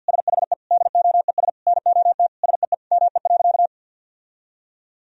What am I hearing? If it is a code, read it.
Morse code, heard as a keyed tone, about 50 wpm, 700 Hz. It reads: SHE DOES NOT SEE ME.
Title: Sentences from Top 1000 Words - Review Part 3 - 50wpm